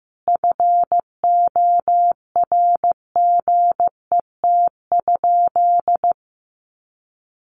Morse code, heard as a keyed tone, about 15 wpm, 700 Hz. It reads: FORGET?
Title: Sentences from Top 1300 Words 15wpm